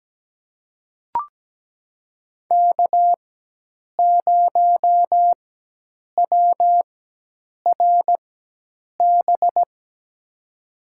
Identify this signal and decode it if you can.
Morse code, heard as a keyed tone, about 17 wpm, spaced as for 8 wpm, 700 Hz. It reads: K0WRB
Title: USA Call Signs - 5 Characters - 4x Character Spacing - 17wpm